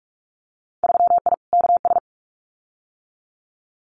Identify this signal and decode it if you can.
Morse code, heard as a keyed tone, about 46 wpm, 700 Hz. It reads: 3I KS
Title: Winter Field Day Exchanges Encore 45wpm